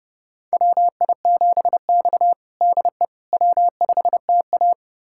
Morse code, heard as a keyed tone, about 32 wpm, 700 Hz. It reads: WI7X DE W5TA
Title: USA Callsign DE Callsign - 4 Characters - 30wpm